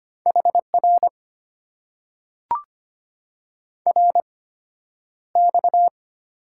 Morse code, heard as a keyed tone, about 25 wpm, spaced as for 11 wpm, 700 Hz. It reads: HR RX
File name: Top 90 QSO Elements - 8x Character Spacing - 25wpm